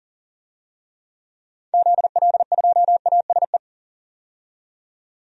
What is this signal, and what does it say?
Morse code, heard as a keyed tone, about 40 wpm, 700 Hz. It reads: ZL2ASE
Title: DXCC-100 Call Signs - 6 Characters - 40wpm